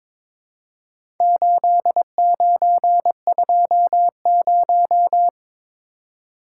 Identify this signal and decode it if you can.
Morse code, heard as a keyed tone, about 22 wpm, 700 Hz. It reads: 8920
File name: practice / 22wpm / Four Letter-Number Combinations - 22wpm